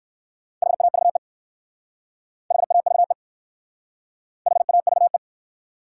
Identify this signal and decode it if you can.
Morse code, heard as a keyed tone, about 54 wpm, 700 Hz. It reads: HAVE HAVE HAVE
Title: Speed-Racing Top 180 QSO Elements 45wpm